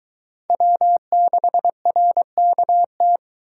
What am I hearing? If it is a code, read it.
Morse code, heard as a keyed tone, about 22 wpm, 700 Hz. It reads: W6RKT